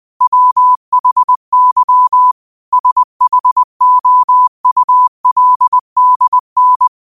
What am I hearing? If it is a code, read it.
Morse code, heard as a keyed tone, about 20 wpm, 1000 Hz. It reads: WHY SHOULDN
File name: 20wpm